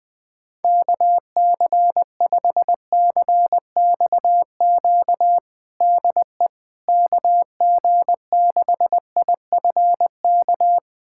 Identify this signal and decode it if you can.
Morse code, heard as a keyed tone, about 20 wpm, 700 Hz. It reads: KC5CXQ DE KG6IFK
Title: USA Callsign DE Callsign 20wpm